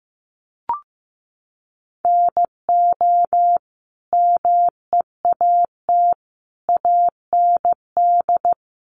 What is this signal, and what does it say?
Morse code, heard as a keyed tone, about 15 wpm, 700 Hz. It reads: NO MEAT AND
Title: Sentences from Top 1500 Words 15wpm